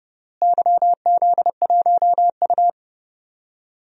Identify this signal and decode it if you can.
Morse code, heard as a keyed tone, about 29 wpm, 700 Hz. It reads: YZ1U
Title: DXCC-100 Call Signs - 4 Characters - Repeated 3x - 30wpm